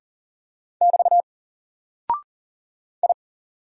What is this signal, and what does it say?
Morse code, heard as a keyed tone, about 40 wpm, 700 Hz. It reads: <BT> I
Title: Single Character (Letter:Number:Punctuation:Prosign) Alt 40wpm